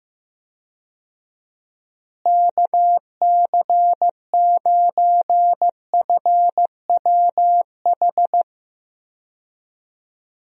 Morse code, heard as a keyed tone, about 15 wpm, 700 Hz. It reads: KC9FWH
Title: USA Call Signs - 6 Characters - Repeated 2x - 15wpm